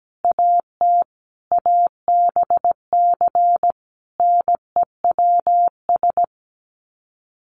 Morse code, heard as a keyed tone, about 17 wpm, 700 Hz. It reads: AT ABC NEWS
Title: Sets of 5 Words 17wpm